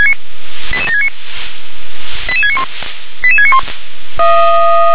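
Five-tone selective call (ZVEI), heard as quick runs of five tones, tones of about 70 ms. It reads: 88769 88691 88691
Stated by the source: Received on SDR, ZVEI five-tone call.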